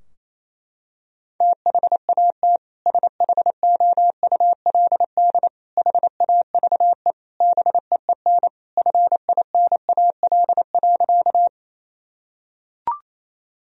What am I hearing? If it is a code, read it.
Morse code, heard as a keyed tone, about 28 wpm, 700 Hz. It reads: THAT SHOULD HAVE BEEN FINAL.
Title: Sentences from Top 1000 Words - Review Part 2 - 28wpm